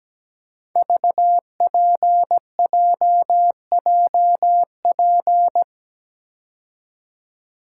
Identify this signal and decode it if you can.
Morse code, heard as a keyed tone, about 17 wpm, 700 Hz. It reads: VPJJP
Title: Code Groups - 5 Letters - 17wpm